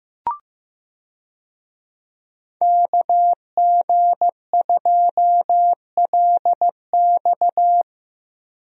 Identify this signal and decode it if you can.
Morse code, heard as a keyed tone, about 15 wpm, 700 Hz. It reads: KG2LX
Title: USA Call Signs - 5 Characters - Repeated 3x - 15wpm